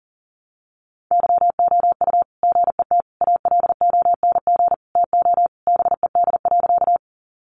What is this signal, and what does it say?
Morse code, heard as a keyed tone, about 40 wpm, 700 Hz. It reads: YOU GET ALONG TO BED.
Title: Sentences from Top 700 Words 40wpm